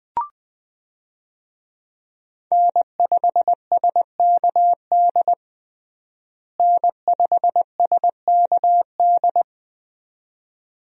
Morse code, heard as a keyed tone, about 20 wpm, 700 Hz. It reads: N5SKD N5SKD
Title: USA Call Signs - 5 Characters - Repeated 2x - 20wpm